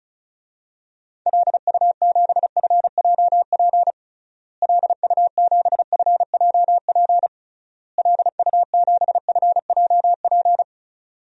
Morse code, heard as a keyed tone, about 36 wpm, 700 Hz. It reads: LU7FJP LU7FJP LU7FJP
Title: DXCC-100 Call Signs - 6 Characters - Repeated 3x - 35wpm